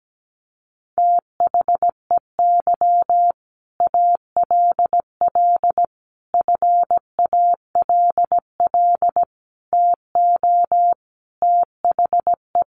Morse code, heard as a keyed tone, about 17 wpm, 700 Hz. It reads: THEY ALL FALL TO THE
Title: Sentences from Top 500 Words 17wpm